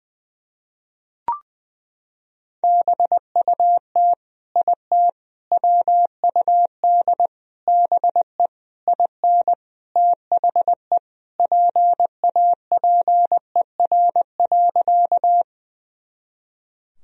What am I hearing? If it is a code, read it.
Morse code, heard as a keyed tone, about 21 wpm, 700 Hz. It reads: BUT IT WUD BE IN THE PAPER.
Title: Sentences from Top 300 Words and Abbreviations 20wpm